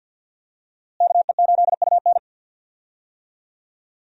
Morse code, heard as a keyed tone, about 52 wpm, 700 Hz. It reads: KE8UN
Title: USA Call Signs - 5 Characters - Repeated 2x - 50wpm